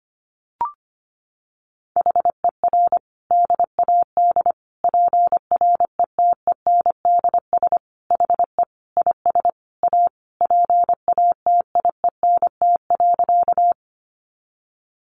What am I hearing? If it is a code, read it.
Morse code, heard as a keyed tone, about 25 wpm, 700 Hz. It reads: HER DAD PRETENDS HE IS A PATIENT.
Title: Additional Easy Phrases 25wpm